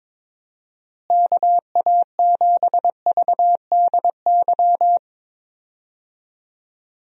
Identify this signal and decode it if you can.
Morse code, heard as a keyed tone, about 22 wpm, 700 Hz. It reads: KA7VDY
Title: USA Call Signs - 6 Characters - Repeated 2x - 22wpm